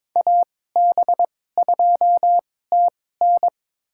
Morse code, heard as a keyed tone, about 22 wpm, spaced as for 16 wpm, 700 Hz. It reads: AB2TN
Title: USA Call Signs - 5 Characters - 2x Character Spacing - 22wpm